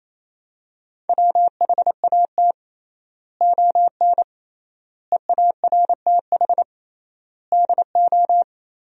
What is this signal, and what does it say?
Morse code, heard as a keyed tone, about 28 wpm, 700 Hz. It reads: WHAT ON EARTH DO
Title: 3x Word-Spacing - Sentences from Top 200 Words - 28wpm